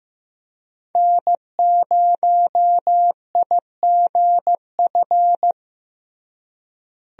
Morse code, heard as a keyed tone, about 15 wpm, 700 Hz. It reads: N0IGF
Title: USA Call Signs - 5 Characters - Repeated 3x - 15wpm